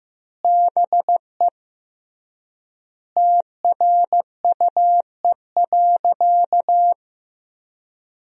Morse code, heard as a keyed tone, about 15 wpm, 700 Hz. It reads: BE TRUE.
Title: 3x Word-Spacing - Sentences from Top 400 Words - 15wpm